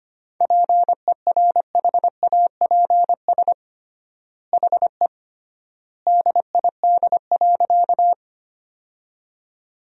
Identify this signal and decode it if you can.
Morse code, heard as a keyed tone, about 25 wpm, 700 Hz. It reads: PERHAPS HE DID.
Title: 3x Word-Spacing - Sentences from Top 500 Words - 25wpm